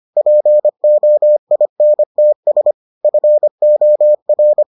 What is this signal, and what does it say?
Morse code, heard as a keyed tone, about 25 wpm, 600 Hz. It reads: POINTS FOR